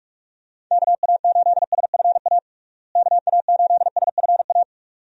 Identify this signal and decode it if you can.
Morse code, heard as a keyed tone, about 46 wpm, 700 Hz. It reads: KA8SFA KA8SFA
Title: USA Call Signs - 6 Characters - Repeated 2x - 45wpm